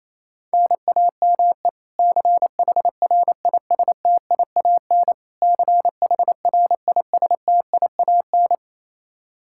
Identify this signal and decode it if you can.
Morse code, heard as a keyed tone, about 28 wpm, 700 Hz. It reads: NAME CHRISTIAN CHRISTIAN
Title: Practice QSO - Exchange 1 - Beginning - 28wpm